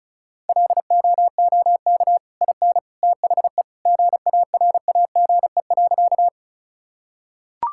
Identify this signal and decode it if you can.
Morse code, heard as a keyed tone, about 34 wpm, 700 Hz. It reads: LOOK IN THE GARAGE.
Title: Sentences from Top 1600 Words 35wpm